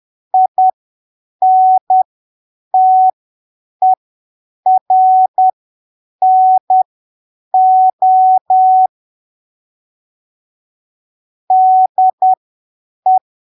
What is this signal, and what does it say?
Morse code, heard as a keyed tone, about 10 wpm, spaced as for 7 wpm, 750 Hz. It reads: INTERNO DE
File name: arquivos / texto / texto 11 10ppm